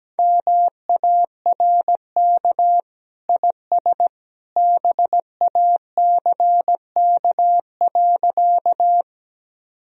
Morse code, heard as a keyed tone, about 17 wpm, 700 Hz. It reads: MARK IS BACK.
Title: Sentences from Top 1000 Words - Review Part 1 - 17wpm